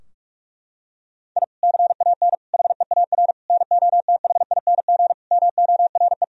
Morse code, heard as a keyed tone, about 44 wpm, 700 Hz. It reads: I CAN HEAR NOTHING MORE
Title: Sentences from Top 1000 Words - Review Part 4 - 45wpm